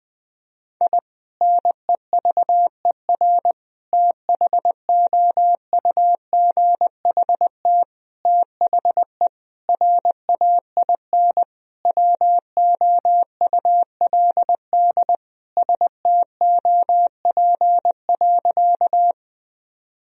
Morse code, heard as a keyed tone, about 20 wpm, 700 Hz. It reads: I NEVER THOUGHT THE RAIN WOULD STOP.
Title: Sentences from Top 1000 Words - Review Part 1 - 20wpm